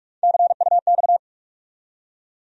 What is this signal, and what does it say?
Morse code, heard as a keyed tone, about 44 wpm, 700 Hz. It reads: CUX
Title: USA Call Sign Postfixes - 3 Letters 45wpm